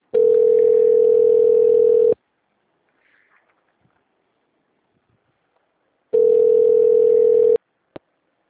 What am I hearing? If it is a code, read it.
Ringback tone (North America), 440 + 480 Hz.